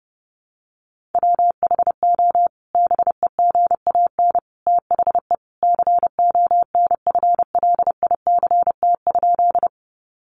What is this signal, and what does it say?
Morse code, heard as a keyed tone, about 30 wpm, 700 Hz. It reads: WHO BEGAN THE CONFLICT?
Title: Sentences from Top 1300 Words 30wpm